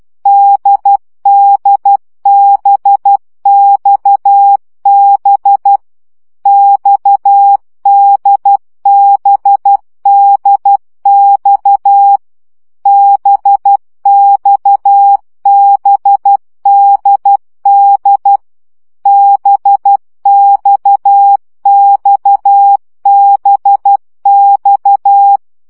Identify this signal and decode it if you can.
Morse code, heard as a keyed tone, about 12 wpm, 800 Hz. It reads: DDBXB XDBDX BXBDD BXXBX
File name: CW gyakorlas d b x 10 baud